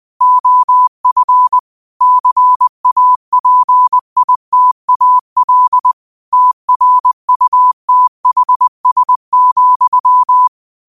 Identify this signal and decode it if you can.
Morse code, heard as a keyed tone, about 20 wpm, 1000 Hz. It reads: OF CAPITAL TRUTHS,
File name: qotdfiles / 20wpm